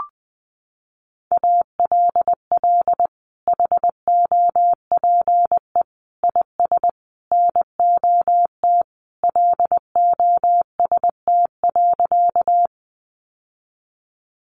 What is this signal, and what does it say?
Morse code, heard as a keyed tone, about 20 wpm, 700 Hz. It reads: ALL HOPE IS NOT LOST.
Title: Sentences from Top 1400 Words 20wpm